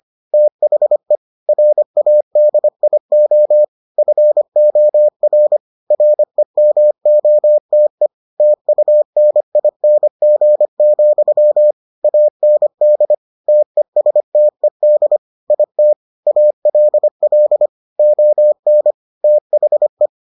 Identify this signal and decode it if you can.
Morse code, heard as a keyed tone, about 25 wpm, 600 Hz. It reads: THE RADIO FOR REMOTE TUNING, AND TESTED IT ALL ON THE